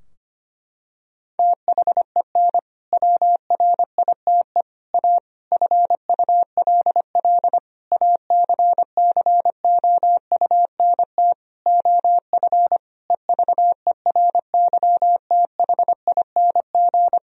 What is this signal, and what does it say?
Morse code, heard as a keyed tone, about 25 wpm, 700 Hz. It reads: THEN WRITE A FULL ACCOUNT OF EVERYTHING
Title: Sentences from Top 1000 Words - Review Part 3 - 25wpm